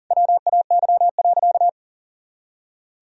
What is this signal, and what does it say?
Morse code, heard as a keyed tone, about 40 wpm, 700 Hz. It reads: WAY.
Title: Sentences from Top 300 Words and Abbreviations 40wpm